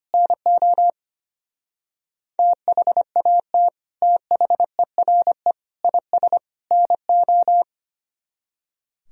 Morse code, heard as a keyed tone, about 31 wpm, 700 Hz. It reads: NO THAT THERE IS NO
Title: Speed-Racing - Sets of 4 Words 25wpm